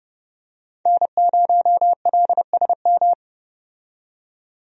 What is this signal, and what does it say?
Morse code, heard as a keyed tone, about 32 wpm, 700 Hz. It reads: N0LSM
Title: USA Call Signs - 5 Characters - 30wpm